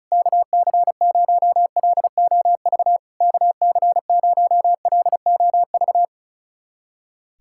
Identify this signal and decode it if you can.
Morse code, heard as a keyed tone, about 36 wpm, 700 Hz. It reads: KC0LOV KC0LOV
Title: SOTA QSO Practice 35wpm